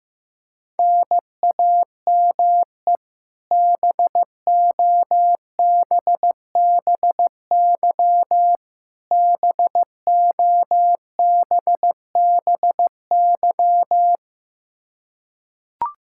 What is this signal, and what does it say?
Morse code, heard as a keyed tone, about 15 wpm, 700 Hz. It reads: NAME BOBBY BOBBY
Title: Practice QSO - Exchange 1 - Beginning - 15wpm